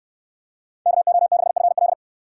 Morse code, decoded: KO6FL

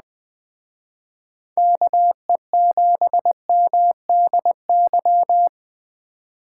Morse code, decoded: KE7MDY

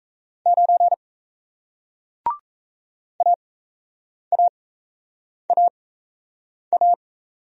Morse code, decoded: 9 AAAA

